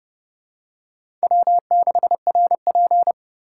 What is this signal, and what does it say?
Morse code, heard as a keyed tone, about 28 wpm, 700 Hz. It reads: W6RP